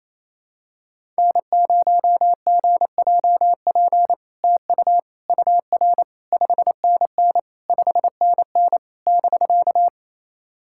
Morse code, heard as a keyed tone, about 28 wpm, 700 Hz. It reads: N0GJP TU UR 5NN 5NN <BK>
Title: SOTA QSO Practice 28wpm